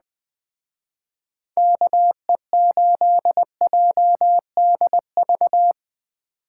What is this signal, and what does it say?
Morse code, heard as a keyed tone, about 20 wpm, 700 Hz. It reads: KE8JDV
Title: Speed Racing - USA 2x3 Call Signs 17wpm